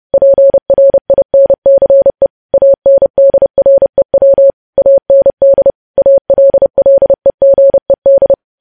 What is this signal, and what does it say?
Morse code, heard as a keyed tone, about 30 wpm, 550 Hz. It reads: PRINCE ANDREW AND ALLEGED
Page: News Headlines in Morse Code at 30 WPM